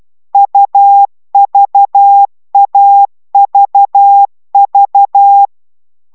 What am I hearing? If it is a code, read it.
Morse code, heard as a keyed tone, about 12 wpm, 800 Hz. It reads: UVAVV